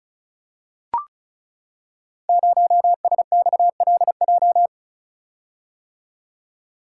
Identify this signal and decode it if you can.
Morse code, heard as a keyed tone, about 36 wpm, 700 Hz. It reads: 0SXLJ
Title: Code Groups - 5 Letter:Numbers - 35wpm